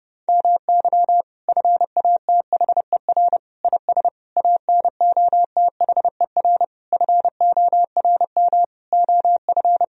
Morse code, heard as a keyed tone, about 32 wpm, 700 Hz. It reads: MY FATHER IS ANOTHER FORM OF
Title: Sentences from Top 1000 Words - Review Part 1 - 30wpm